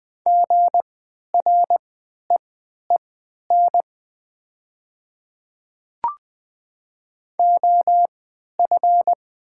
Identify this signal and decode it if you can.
Morse code, heard as a keyed tone, about 20 wpm, spaced as for 11 wpm, 700 Hz. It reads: GREEN OF
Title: 3x Character Spacing - Top 500 Words - 20wpm